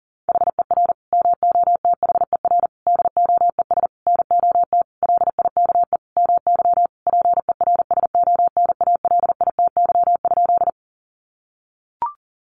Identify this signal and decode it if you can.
Morse code, heard as a keyed tone, about 40 wpm, 700 Hz. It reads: HER MOTHER DOES NOT LIKE MY PERSONALITY?
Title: Sentences from Top 1400 Words 40wpm